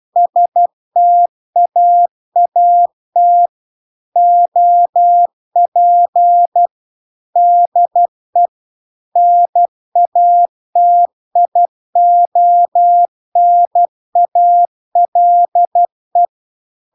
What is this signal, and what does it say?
Morse code, heard as a keyed tone, about 12 wpm, 700 Hz. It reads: STAAT OP DE NATIONALE
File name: cw-cursus-VERON-a03